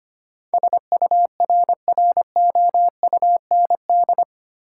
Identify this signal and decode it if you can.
Morse code, heard as a keyed tone, about 25 wpm, 700 Hz. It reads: SURROUND
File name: Eight Letter Words - 25wpm